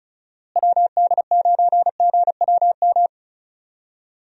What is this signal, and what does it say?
Morse code, heard as a keyed tone, about 36 wpm, 700 Hz. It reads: WD9GWM
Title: USA Call Signs - 6 Characters - Repeated 2x - 35wpm